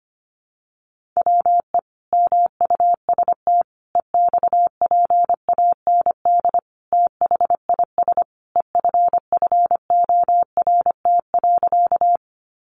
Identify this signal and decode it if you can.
Morse code, heard as a keyed tone, about 25 wpm, 700 Hz. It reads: WE MUST EXPAND THIS EFFORT.
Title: Sentences from Top 1400 Words 25wpm